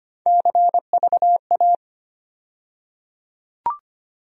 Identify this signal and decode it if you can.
Morse code, heard as a keyed tone, about 25 wpm, 700 Hz. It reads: CVA
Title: USA Call Sign Postfixes - 3 Letters 25wpm